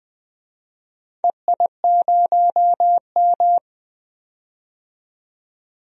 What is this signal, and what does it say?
Morse code, heard as a keyed tone, about 20 wpm, 700 Hz. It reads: EI0M
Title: DXCC-100 Call Signs - 4 Characters - 20wpm